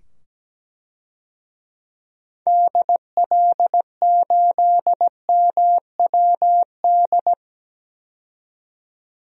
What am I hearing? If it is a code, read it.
Morse code, heard as a keyed tone, about 17 wpm, 700 Hz. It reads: DL8MWD
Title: DXCC-100 Call Signs - 6 Characters - 17wpm